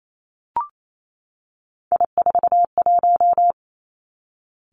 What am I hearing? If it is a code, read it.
Morse code, heard as a keyed tone, about 28 wpm, 700 Hz. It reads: I41
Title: Three Challenging Letter-Number Combinations 28wpm